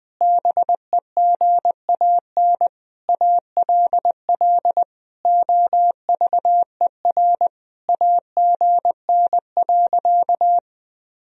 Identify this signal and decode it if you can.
Morse code, heard as a keyed tone, about 20 wpm, 700 Hz. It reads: BEGAN ALL OVER AGN.
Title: Sentences from Top 500 Words and Abbreviations 20wpm